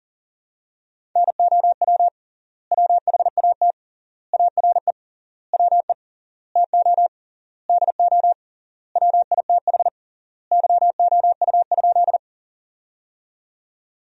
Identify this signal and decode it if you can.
Morse code, heard as a keyed tone, about 40 wpm, 700 Hz. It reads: NOW WHAT ARE WE TO DO WITH YOU?
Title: 3x Word-Spacing - Sentences from Top 100 Words - 40wpm